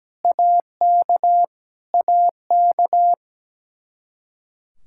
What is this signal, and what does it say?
Morse code, heard as a keyed tone, about 17 wpm, 700 Hz. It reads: AK AK
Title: Practice QSO - Exchange 1 - Beginning - 17wpm